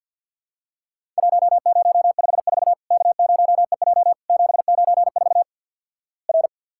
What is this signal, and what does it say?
Morse code, heard as a keyed tone, about 50 wpm, 700 Hz. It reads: 1054 K0EJ 794 R
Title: Worked All Europe DX Contest - QTC Exchange - 50wpm